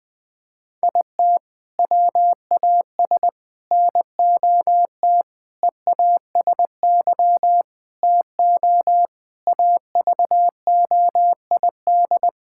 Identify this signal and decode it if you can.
Morse code, heard as a keyed tone, about 20 wpm, 700 Hz. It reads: IT WAS NOT EASY TO AVOID
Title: Sentences from Top 1000 Words - Review Part 6 - 20wpm